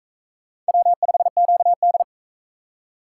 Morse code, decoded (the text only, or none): W5QD